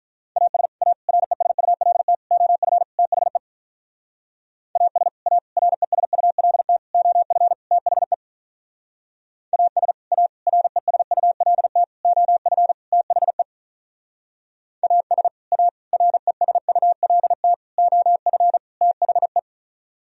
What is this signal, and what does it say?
Morse code, heard as a keyed tone, about 42 wpm, 700 Hz. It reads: AS A RESULT OF THE AS A RESULT OF THE AS A RESULT OF THE AS A RESULT OF THE
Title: Speed-Racing - Sets of 5 Words 35wpm